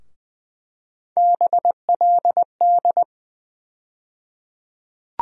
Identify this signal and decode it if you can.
Morse code, heard as a keyed tone, about 20 wpm, 700 Hz. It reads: BLD
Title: USA Call Sign Postfixes - 3 Letters 20wpm